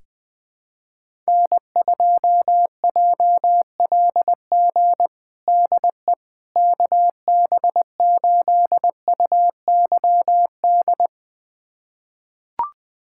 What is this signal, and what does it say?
Morse code, heard as a keyed tone, about 21 wpm, 700 Hz. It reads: N2JLG DE KB8UYD
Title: USA Callsign DE Callsign 20wpm